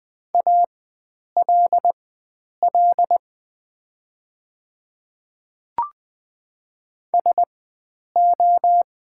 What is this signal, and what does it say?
Morse code, heard as a keyed tone, about 20 wpm, spaced as for 9 wpm, 700 Hz. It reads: ALL SO
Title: 4x Character Spacing - Top 100 Words - 20wpm